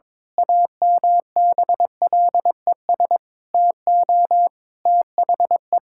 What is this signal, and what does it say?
Morse code, heard as a keyed tone, about 22 wpm, 700 Hz. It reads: AMBLES TO THE
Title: Pangram Challenge v2 - 22wpm